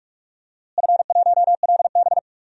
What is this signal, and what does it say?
Morse code, heard as a keyed tone, about 42 wpm, 700 Hz. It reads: F1LB
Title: DXCC-100 Call Signs - 4 Characters - Repeated 2x - 45wpm